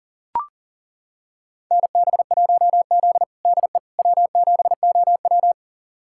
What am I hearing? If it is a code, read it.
Morse code, heard as a keyed tone, about 40 wpm, 700 Hz. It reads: NB1Z DE W7OW
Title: USA Callsign DE Callsign - 4 Characters - 40wpm